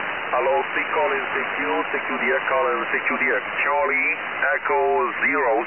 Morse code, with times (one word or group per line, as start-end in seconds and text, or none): none